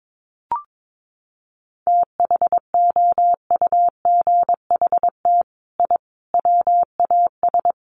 Morse code, heard as a keyed tone, about 22 wpm, 700 Hz. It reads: THOUGHT I WAS
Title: Sets of 5 Words Encore2 22wpm